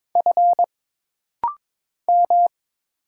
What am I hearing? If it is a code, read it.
Morse code, heard as a keyed tone, about 22 wpm, 700 Hz. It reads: F M
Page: This rapid-fire timing is geared to push your ICR (Instant Character Recognition) skills.